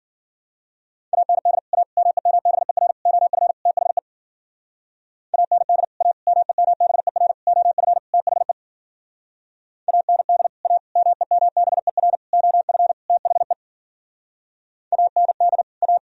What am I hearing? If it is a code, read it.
Morse code, heard as a keyed tone, about 52 wpm, 700 Hz. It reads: AND A MEMBER OF THE AND A MEMBER OF THE AND A MEMBER OF THE AND A